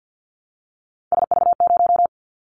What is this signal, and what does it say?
Morse code, heard as a keyed tone, about 50 wpm, 700 Hz. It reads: SV0
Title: Callsign Prefixes for Top 50 DXCC - 50wpm